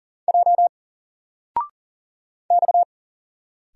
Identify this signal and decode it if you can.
Morse code, heard as a keyed tone, about 38 wpm, 700 Hz. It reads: J X
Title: Single Character (Letter:Number:Punctuation:Prosign) 40wpm